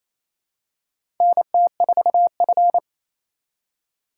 Morse code, heard as a keyed tone, about 28 wpm, 700 Hz. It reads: NT4F
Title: USA Call Signs - 4 Characters - 2x Character Spacing - 28wpm